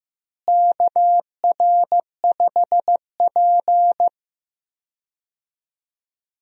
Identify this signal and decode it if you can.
Morse code, heard as a keyed tone, about 15 wpm, 700 Hz. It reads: KR5P
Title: USA Call Signs - 4 Characters - Repeated 3x - 15wpm